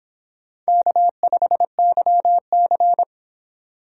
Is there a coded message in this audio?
Yes: Morse code from 0.7 to 3.0 s, heard as a keyed tone, about 24 wpm, 700 Hz. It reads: K5YC